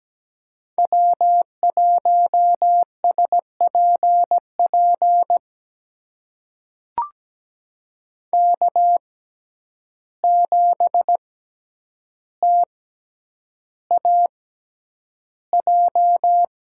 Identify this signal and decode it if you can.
Morse code, heard as a keyed tone, about 17 wpm, spaced as for 9 wpm, 700 Hz. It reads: W1SPP K7TAJ
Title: USA Call Signs - 5 Characters - 6x Character Spacing - 17wpm